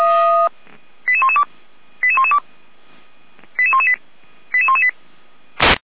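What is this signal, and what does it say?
Five-tone selective call (ZVEI), heard as quick runs of five tones, tones of about 70 ms. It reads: 88112 88112 88118 88118
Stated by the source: Five-tone ZVEI selective call received by SDR.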